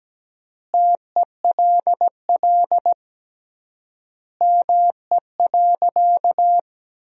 Morse code, heard as a keyed tone, about 17 wpm, 700 Hz. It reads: TELL ME.
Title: 3x Word-Spacing - Sentences from Top 300 Words - 17wpm